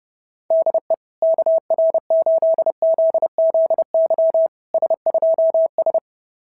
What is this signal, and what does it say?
Morse code, heard as a keyed tone, about 32 wpm, 690 Hz. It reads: DE KR8ZZY S2S
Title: SOTA QSO Practice 30wpm